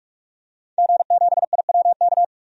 Morse code, heard as a keyed tone, about 46 wpm, 700 Hz. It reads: G7IWK
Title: DXCC-100 Call Signs - 5 Characters - 45wpm